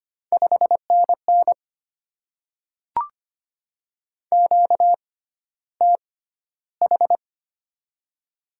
Morse code, heard as a keyed tone, about 25 wpm, spaced as for 13 wpm, 700 Hz. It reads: 5NN QTH
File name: Top 20 QSO Elements - 6x Character Spacing - 25wpm